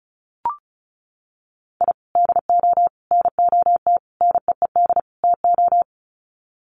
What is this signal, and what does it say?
Morse code, heard as a keyed tone, about 34 wpm, 700 Hz. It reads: I DO NOT NEED TO
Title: Sets of 5 Words 35wpm